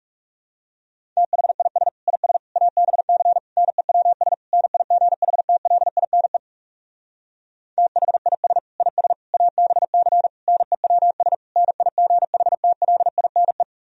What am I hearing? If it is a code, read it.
Morse code, heard as a keyed tone, about 37 wpm, 700 Hz. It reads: THIS IS ABC NEWS NIGHTLINE THIS IS ABC NEWS NIGHTLINE
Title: Speed-Racing - Sets of 5 Words 30wpm